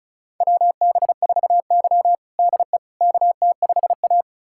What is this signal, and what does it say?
Morse code, heard as a keyed tone, about 36 wpm, 700 Hz. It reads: WB4Y DE KT5A
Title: USA Callsign DE Callsign - 4 Characters - 35wpm